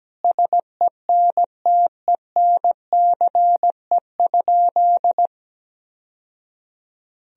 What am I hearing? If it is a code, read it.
Morse code, heard as a keyed tone, about 17 wpm, 700 Hz. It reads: SENTENCE?
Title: Sentences from Top 1000 Words - Review Part 1 - 17wpm